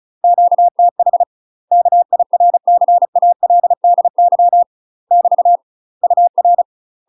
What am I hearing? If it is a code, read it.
Morse code, heard as a keyed tone, about 36 wpm, 700 Hz. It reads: QTH KIRCALDY <BT> UR
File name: UK MASTER QSOs TRAINING / 35WPM / UK MASTER QSOs 0035